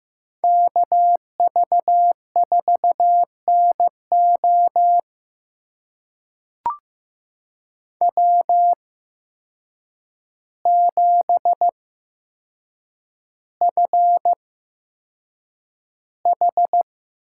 Morse code, heard as a keyed tone, about 15 wpm, spaced as for 7 wpm, 700 Hz. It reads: KV4NO W7FH
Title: USA Call Signs - 5 Characters - 8x Character Spacing - 15wpm